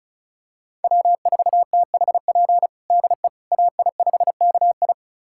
Morse code, heard as a keyed tone, about 36 wpm, 700 Hz. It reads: W4THP DE AI5KI
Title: USA Callsign DE Callsign - 5 Characters - 35wpm